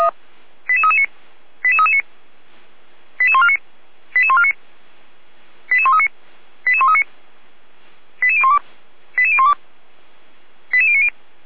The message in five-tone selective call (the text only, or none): 88339 88339 88149 88149 88139 88139 88012 88012 88090